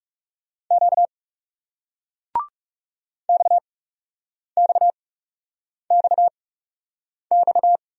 Morse code, heard as a keyed tone, about 38 wpm, 700 Hz. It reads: Q X X X X